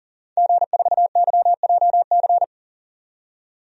Morse code, heard as a keyed tone, about 40 wpm, 700 Hz. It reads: G4YJC
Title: DXCC-100 Call Signs - 5 Characters - 40wpm